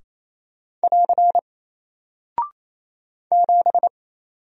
Morse code, heard as a keyed tone, about 28 wpm, 700 Hz. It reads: <AR> 7
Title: Single Character (Letter:Number:Punctuation:Prosign) Alt 28wpm